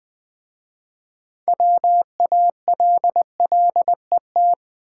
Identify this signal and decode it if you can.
Morse code, heard as a keyed tone, about 20 wpm, 700 Hz. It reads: WALLET
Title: Easy Words with Hints - 20wpm